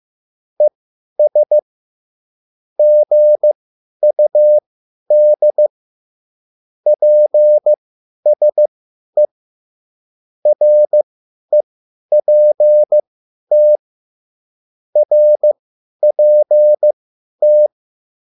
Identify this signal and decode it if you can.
Morse code, heard as a keyed tone, about 15 wpm, spaced as for 10 wpm, 600 Hz. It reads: ES GUD PSE REPT RPT